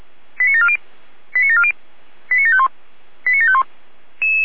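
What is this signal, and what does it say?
Five-tone selective call (ZVEI), heard as quick runs of five tones, tones of about 70 ms. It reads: 89740 89740 89751 89751
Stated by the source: ZVEI five-tone selective call received on SDR, code 89590.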